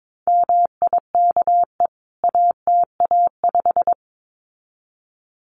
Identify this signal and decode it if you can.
Morse code, heard as a keyed tone, about 22 wpm, 700 Hz. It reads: MIKE ATA5
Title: CWT Exchange 22wpm